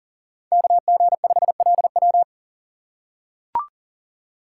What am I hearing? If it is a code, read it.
Morse code, heard as a keyed tone, about 40 wpm, 700 Hz. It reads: KG5LW
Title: USA Call Signs - 5 Characters - 2x Character Spacing - 40wpm